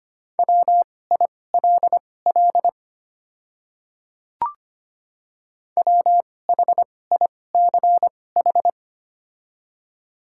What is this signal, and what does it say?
Morse code, heard as a keyed tone, about 25 wpm, spaced as for 18 wpm, 700 Hz. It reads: WILL WHICH
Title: 2x Character Spacing - Top 100 Words - 25wpm